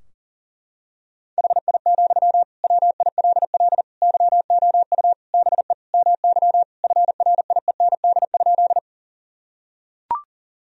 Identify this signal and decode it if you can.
Morse code, heard as a keyed tone, about 40 wpm, 700 Hz. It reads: HI, WILL YOU BE MY FRIEND?
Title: Sentences from Top 1000 Words - Review Part 3 - 40wpm